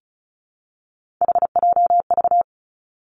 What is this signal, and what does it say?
Morse code, heard as a keyed tone, about 36 wpm, 700 Hz. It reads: HJV
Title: Three Challenging Letter-Number Combinations 35wpm